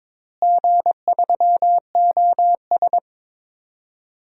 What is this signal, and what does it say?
Morse code, heard as a keyed tone, about 22 wpm, 700 Hz. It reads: G3OS
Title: DXCC-100 Call Signs - 4 Characters - Repeated 2x - 22wpm